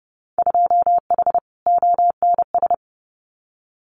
Morse code, heard as a keyed tone, about 30 wpm, 700 Hz. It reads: 2H ONS
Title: Winter Field Day Exchanges Encore 30wpm